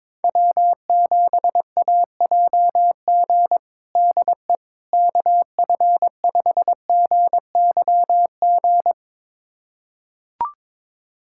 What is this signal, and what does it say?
Morse code, heard as a keyed tone, about 22 wpm, 700 Hz. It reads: W7AJG DE KF5GYG
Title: USA Callsign DE Callsign 22wpm